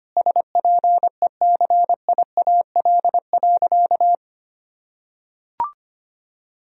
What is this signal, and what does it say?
Morse code, heard as a keyed tone, about 25 wpm, 700 Hz. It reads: SPECIAL.
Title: Sentences from Top 1000 Words - Review Part 3 - 25wpm